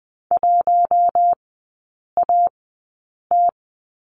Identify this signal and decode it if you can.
Morse code, heard as a keyed tone, about 20 wpm, spaced as for 8 wpm, 700 Hz. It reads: 1AT
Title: 053 Lesson 10e - Letter r - Three Characters 20wpm